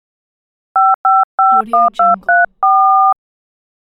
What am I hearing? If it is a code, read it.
DTMF, heard as telephone touch tones, tones of about 210 ms. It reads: 5551234